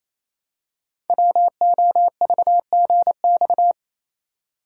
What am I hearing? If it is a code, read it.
Morse code, heard as a keyed tone, about 29 wpm, 700 Hz. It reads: WOVGX
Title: Code Groups - 5 Letters - 28wpm